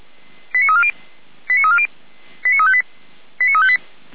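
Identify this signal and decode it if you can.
Five-tone selective call (ZVEI), heard as quick runs of five tones, tones of about 70 ms. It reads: 89340 89340 89358 89358
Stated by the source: ZVEI five-tone call 89330 received by SDR.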